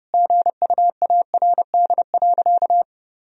Morse code, heard as a keyed tone, about 30 wpm, 700 Hz. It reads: GUARD.
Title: Sentences from Top 1000 Words - Review Part 4 - 30wpm